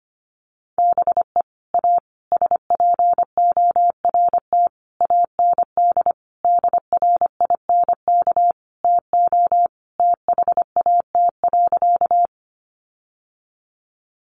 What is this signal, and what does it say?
Morse code, heard as a keyed tone, about 25 wpm, 700 Hz. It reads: BE A SPORT AND DRINK TO THAT.
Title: Sentences from Top 1500 Words 25wpm